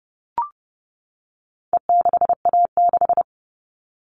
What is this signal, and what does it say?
Morse code, heard as a keyed tone, about 30 wpm, 700 Hz. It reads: E6A6
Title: All Cut Numbers - 4-Digit 30wpm